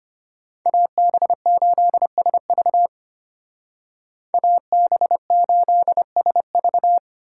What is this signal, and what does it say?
Morse code, heard as a keyed tone, about 30 wpm, 700 Hz. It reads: AB8SV AB8SV
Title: Speed Racing - USA 2x2 Call Signs 25wpm